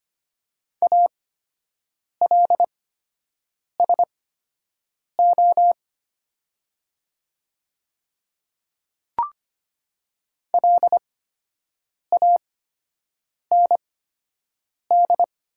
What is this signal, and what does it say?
Morse code, heard as a keyed tone, about 25 wpm, spaced as for 7 wpm, 690 Hz. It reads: ALSO LAND
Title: 8x Character Spacing - Top 500 Words - 25wpm